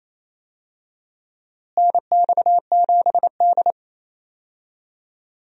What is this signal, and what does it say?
Morse code, heard as a keyed tone, about 27 wpm, 700 Hz. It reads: NX7D
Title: USA Call Signs - 4 Characters - Repeated 3x - 28wpm